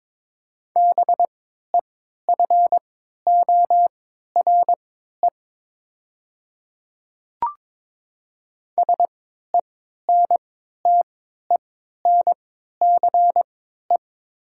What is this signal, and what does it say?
Morse code, heard as a keyed tone, about 22 wpm, spaced as for 12 wpm, 700 Hz. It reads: BEFORE SENTENCE
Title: 3x Character Spacing - Top 200 Words - 22wpm